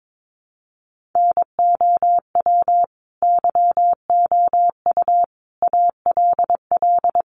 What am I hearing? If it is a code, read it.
Morse code, heard as a keyed tone, about 22 wpm, 700 Hz. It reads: NOW YOU ALL
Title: Sentences from Top 1200 Words 22wpm